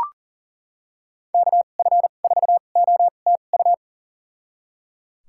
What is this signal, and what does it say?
Morse code, heard as a keyed tone, about 38 wpm, spaced as for 28 wpm, 700 Hz. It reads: KF4OTU
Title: USA Call Signs - 6 Characters - 2x Character Spacing - 40wpm